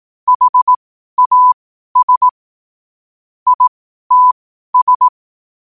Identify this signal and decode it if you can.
Morse code, heard as a keyed tone, about 18 wpm, spaced as for 12 wpm, 1000 Hz. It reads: HAS ITS
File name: qotdfiles / 12wpm